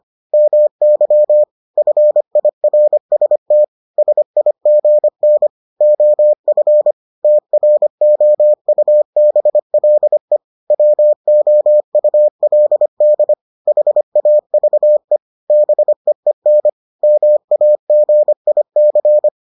Morse code, heard as a keyed tone, about 25 wpm, 600 Hz. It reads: MY FIRST SIGN OF TROUBLE WOULD HAVE BEEN MAGIC